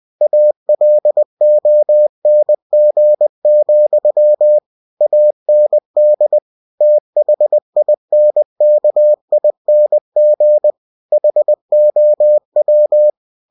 Morse code, heard as a keyed tone, about 20 wpm, 600 Hz. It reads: ALONG, AND THINKING HOW